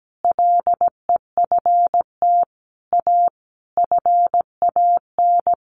Morse code, heard as a keyed tone, about 17 wpm, 700 Hz. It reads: LEFT A FAN
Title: Sentences from Top 1300 Words 17wpm